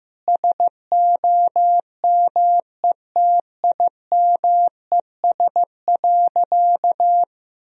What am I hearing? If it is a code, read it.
Morse code, heard as a keyed tone, about 16 wpm, 700 Hz. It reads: SOMETIMES.
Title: Sentences from Top 1000 Words - Review Part 3 - 15wpm